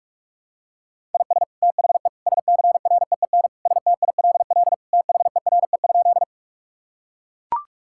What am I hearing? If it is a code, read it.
Morse code, heard as a keyed tone, about 42 wpm, 700 Hz. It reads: IS THE SCREEN STILL THERE?
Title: Sentences from Top 1000 Words - Review Part 6 - 45wpm